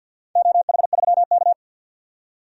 Morse code, heard as a keyed tone, about 51 wpm, 700 Hz. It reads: OH3K
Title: DXCC-100 Call Signs - 4 Characters - 50wpm